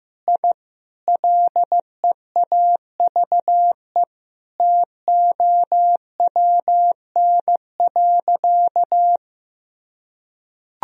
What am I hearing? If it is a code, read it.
Morse code, heard as a keyed tone, about 15 wpm, 700 Hz. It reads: I LEAVE TOWN.
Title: Sentences from Top 1000 Words - Review Part 2 - 15wpm